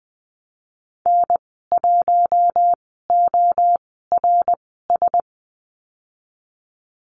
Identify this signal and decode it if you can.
Morse code, heard as a keyed tone, about 20 wpm, spaced as for 15 wpm, 700 Hz. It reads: N1ORS
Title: 076 Lesson 13g - Letter L - Call Signs 20wpm